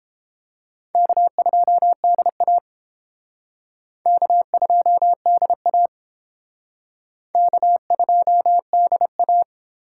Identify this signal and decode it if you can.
Morse code, heard as a keyed tone, about 30 wpm, 700 Hz. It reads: K2DA K2DA K2DA